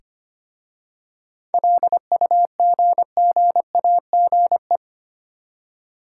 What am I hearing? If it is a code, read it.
Morse code, heard as a keyed tone, about 25 wpm, 700 Hz. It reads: LUGGAGE
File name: practice / 25wpm / Easy Words with Hints - 25wpm